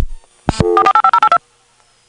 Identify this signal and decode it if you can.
DTMF, heard as telephone touch tones, tones of about 80 ms. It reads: 485082